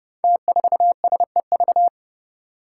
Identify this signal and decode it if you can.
Morse code, heard as a keyed tone, about 30 wpm, 700 Hz. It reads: T4SEV
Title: Five Challenging Letter-Number Combinations 30wpm